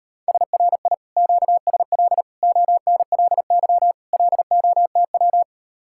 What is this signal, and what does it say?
Morse code, heard as a keyed tone, about 39 wpm, 700 Hz. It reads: SRI QSL ONLY LOTW